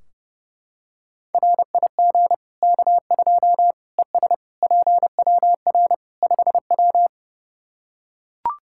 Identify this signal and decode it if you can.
Morse code, heard as a keyed tone, about 32 wpm, 700 Hz. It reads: RIG K2 ES PWR 5W
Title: Practice QSO - Exchange 2 - Info - 30wpm